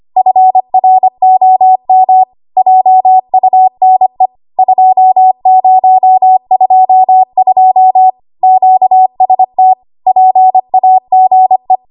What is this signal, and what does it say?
Morse code, heard as a keyed tone, about 25 wpm, 750 Hz. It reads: FROM JUNE 2022 QST PAGE
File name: Archive / 25 WPM / 25WPM